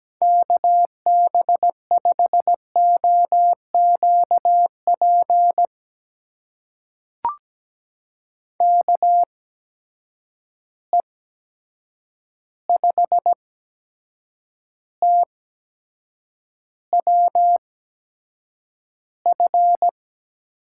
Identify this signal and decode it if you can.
Morse code, heard as a keyed tone, about 17 wpm, spaced as for 7 wpm, 700 Hz. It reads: KB5OQP KE5TWF
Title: USA Call Signs - 6 Characters - 8x Character Spacing - 17wpm